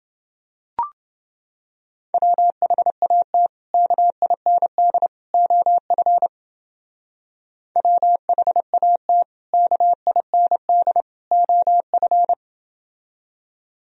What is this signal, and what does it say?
Morse code, heard as a keyed tone, about 28 wpm, 700 Hz. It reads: WHAT KIND OF WHAT KIND OF